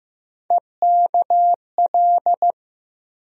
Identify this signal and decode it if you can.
Morse code, heard as a keyed tone, about 15 wpm, 700 Hz. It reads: EKL
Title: USA Call Sign Postfixes - 3 Letters 15wpm